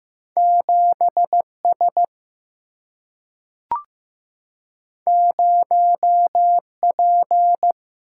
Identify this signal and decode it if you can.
Morse code, heard as a keyed tone, about 15 wpm, 700 Hz. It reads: 7S 0P
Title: Single-Letter Single-Number 15wpm